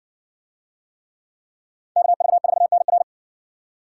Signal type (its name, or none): Morse code